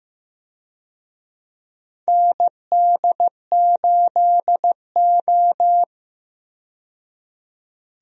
Morse code, heard as a keyed tone, about 15 wpm, 700 Hz. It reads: ND8O
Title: USA Call Signs - 4 Characters - 15wpm